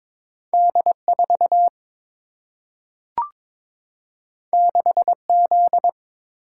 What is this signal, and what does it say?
Morse code, heard as a keyed tone, about 22 wpm, 700 Hz. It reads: D 4 6 Z
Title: Single-Letter Single-Number 22wpm